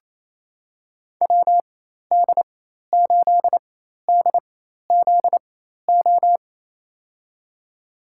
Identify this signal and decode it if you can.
Morse code, heard as a keyed tone, about 28 wpm, spaced as for 13 wpm, 700 Hz. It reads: WD8DZO
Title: USA Call Signs - 4x Character Spacing - 28wpm